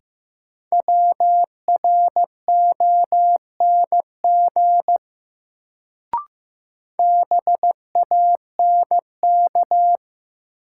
Morse code, heard as a keyed tone, about 15 wpm, 700 Hz. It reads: WRONG BANK
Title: Sentences from Top 900 Words - New Word Drill - 15wpm